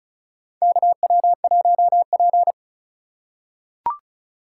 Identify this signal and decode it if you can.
Morse code, heard as a keyed tone, about 36 wpm, 700 Hz. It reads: KW1P
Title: USA Call Signs - 2x Character Spacing - 35wpm